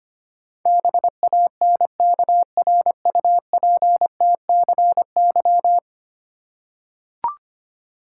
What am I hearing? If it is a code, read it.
Morse code, heard as a keyed tone, about 25 wpm, 700 Hz. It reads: BANKRUPTCY
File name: 10 Letter Words - First 3 Letters Given - 25wpm